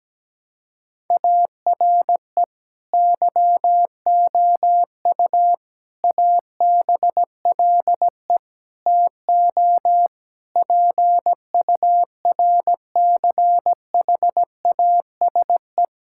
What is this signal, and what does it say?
Morse code, heard as a keyed tone, about 17 wpm, 700 Hz. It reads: ARE YOU ABLE TO PURCHASE
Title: Sentences from Top 1000 Words - Review Part 5 - 17wpm